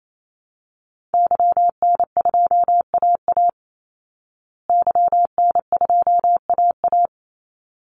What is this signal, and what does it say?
Morse code, heard as a keyed tone, about 28 wpm, 700 Hz. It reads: YN2AA YN2AA
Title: Call Signs Repeated 3x - 28wpm